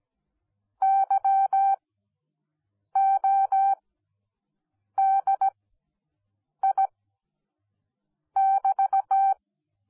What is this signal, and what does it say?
Morse code, heard as a keyed tone, about 17 wpm, spaced as for 6 wpm, 780 Hz. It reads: YODI<BT>
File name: Onlinelehrgang / morsen / DJ4UF Morsen10